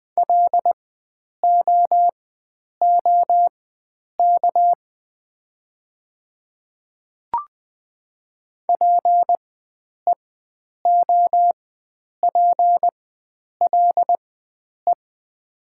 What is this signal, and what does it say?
Morse code, heard as a keyed tone, about 20 wpm, spaced as for 9 wpm, 690 Hz. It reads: LOOK PEOPLE
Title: 4x Character Spacing - Top 100 Words - 20wpm